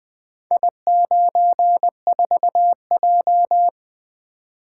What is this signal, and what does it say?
Morse code, heard as a keyed tone, about 21 wpm, 700 Hz. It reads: I94J
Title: Four Letter-Number Combinations - 20wpm